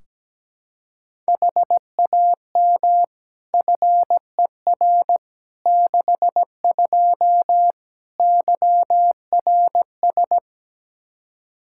Morse code, heard as a keyed tone, about 17 wpm, 700 Hz. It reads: HAM FER 62 YRS
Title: K1USN SST QSO Practice 17wpm